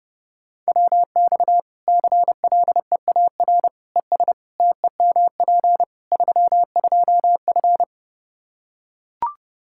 Morse code, heard as a keyed tone, about 32 wpm, 700 Hz. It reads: WX CLEAR ES TEMP 32F
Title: Practice QSO - Exchange 2 - Info - 30wpm